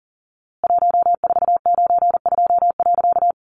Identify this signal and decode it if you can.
Morse code, heard as a keyed tone, about 40 wpm, 700 Hz. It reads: 1492.
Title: A Sound Of Thunder 40wpm